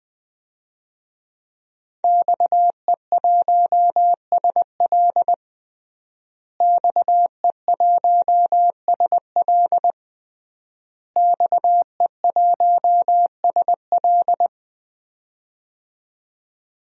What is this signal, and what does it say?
Morse code, heard as a keyed tone, about 20 wpm, 700 Hz. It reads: XE1SL XE1SL XE1SL
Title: DXCC-100 Call Signs - 5 Characters - Repeated 3x - 20wpm